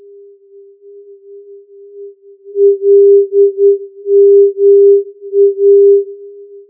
Morse code, heard as a keyed tone, about 9 wpm, 400 Hz. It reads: LMA
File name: filtered morse